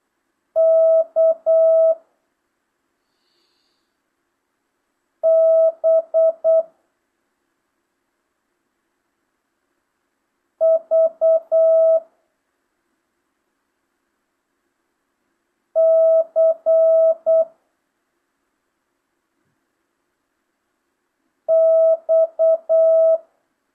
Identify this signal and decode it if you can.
Morse code, heard as a keyed tone, about 8 wpm, 650 Hz. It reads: K B V C X
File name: jh8cbh / 2025cw